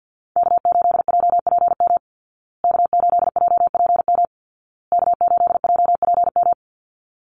Morse code, heard as a keyed tone, about 50 wpm, 700 Hz. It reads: K8JPM K8JPM K8JPM
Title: Call Signs Repeated 2x - 50wpm